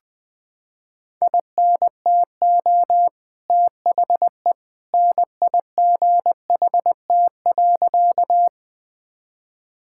Morse code, heard as a keyed tone, about 20 wpm, 700 Hz. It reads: INTO THE NIGHT.
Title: Sentences with 2-Word Phrases - 20wpm